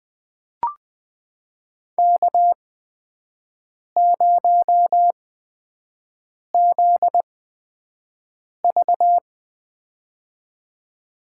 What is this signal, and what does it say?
Morse code, heard as a keyed tone, about 20 wpm, spaced as for 5 wpm, 700 Hz. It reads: K0ZV
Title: USA Call Signs - 4 Characters - 8x Character Spacing - 20wpm